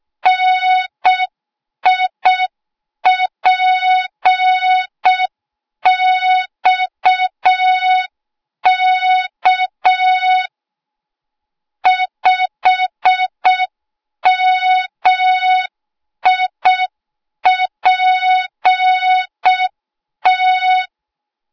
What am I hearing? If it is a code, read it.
Morse code, heard as a keyed tone, about 6 wpm, 3700 Hz. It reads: NIPXK 5MIPT
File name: Morze-HA2MN / hang / 08KXC HA2MN